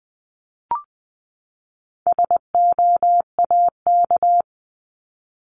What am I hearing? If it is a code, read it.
Morse code, heard as a keyed tone, about 20 wpm, 700 Hz. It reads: SOAK
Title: 260 Lesson 42e - Words 20-20wpm